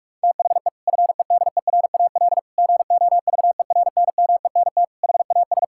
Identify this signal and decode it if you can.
Morse code, heard as a keyed tone, about 46 wpm, 700 Hz. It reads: THE FEDERAL GOVERNMENT HAS
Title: Sentences from Top 1000 Words - Review Part 6 - 45wpm